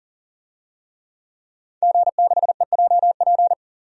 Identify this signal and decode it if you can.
Morse code, heard as a keyed tone, about 40 wpm, 700 Hz. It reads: G6EJP
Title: DXCC-100 Call Signs - 5 Characters - 40wpm